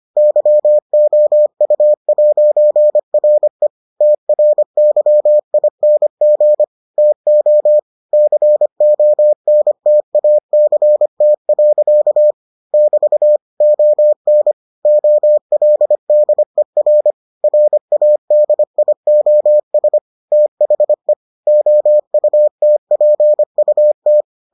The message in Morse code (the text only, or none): YOU'RE TRYING TO CONTACT. <BT> ON OLDER RADIOS THE OUTPUT